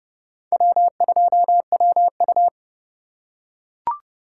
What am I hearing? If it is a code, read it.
Morse code, heard as a keyed tone, about 30 wpm, 700 Hz. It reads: W2WU
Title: USA Call Signs - 2x Character Spacing - 30wpm